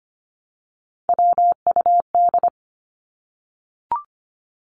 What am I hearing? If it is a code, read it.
Morse code, heard as a keyed tone, about 25 wpm, 700 Hz. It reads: WUD
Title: Abbreviations 25wpm